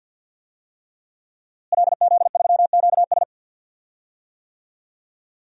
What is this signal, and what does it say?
Morse code, heard as a keyed tone, about 47 wpm, 700 Hz. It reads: LZ3QS
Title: DXCC-100 Call Signs - 5 Characters - Repeated 3x - 50wpm